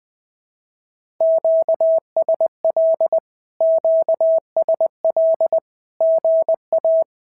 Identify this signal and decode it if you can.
Morse code, heard as a keyed tone, about 20 wpm, 650 Hz. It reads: QSL QSL GA
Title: SOTA QSO Practice 20wpm